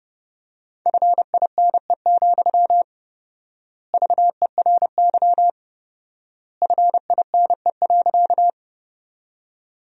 Morse code, heard as a keyed tone, about 30 wpm, 700 Hz. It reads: FINE, VERY FINE.
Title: 4x Word-Spacing - Sentences from Top 500 Words - 30wpm